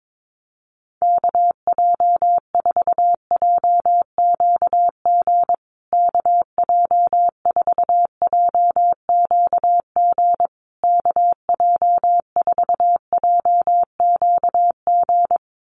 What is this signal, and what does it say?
Morse code, heard as a keyed tone, about 22 wpm, 700 Hz. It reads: KJ4JQG KJ4JQG KJ4JQG
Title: USA Call Signs Repeated 3x 22wpm